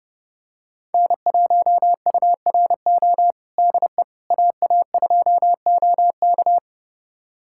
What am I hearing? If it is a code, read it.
Morse code, heard as a keyed tone, about 30 wpm, 700 Hz. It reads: N1URO DE AA2OK
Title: USA Callsign DE Callsign - 5 Characters - 30wpm